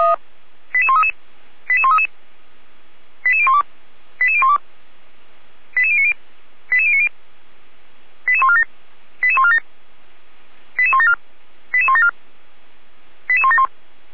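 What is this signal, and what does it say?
Five-tone selective call (ZVEI), heard as quick runs of five tones, tones of about 70 ms. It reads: 88133 88133 88012 88012 88090 88090 88147 88147 88174 88174 88171
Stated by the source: ZVEI five-tone call received by SDR.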